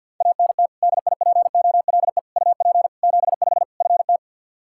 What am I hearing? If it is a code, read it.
Morse code, heard as a keyed tone, about 50 wpm, 700 Hz. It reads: ANT DIPOLE UP 75 FT
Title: Practice QSO - Exchange 2 - Info - 50wpm